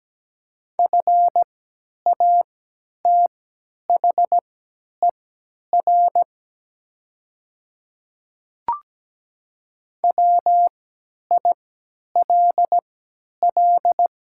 Morse code, heard as a keyed tone, about 17 wpm, spaced as for 10 wpm, 700 Hz. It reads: FATHER WILL
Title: 3x Character Spacing - Top 300 Words - 17wpm